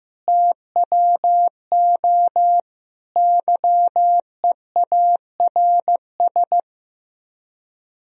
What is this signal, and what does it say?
Morse code, heard as a keyed tone, about 15 wpm, 700 Hz. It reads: TWO YEARS
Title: Speed-Racing - Sets of 5 Words 15wpm